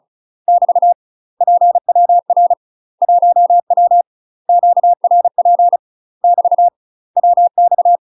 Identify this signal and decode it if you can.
Morse code, heard as a keyed tone, about 34 wpm, 700 Hz. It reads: <BT> PWR 1W QRP <BT> WX
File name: UK MASTER QSOs TRAINING / 35WPM / UK MASTER QSOs 0037